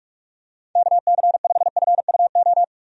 Morse code, heard as a keyed tone, about 46 wpm, 700 Hz. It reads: KC5FUO
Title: USA Call Signs - 6 Characters - 45wpm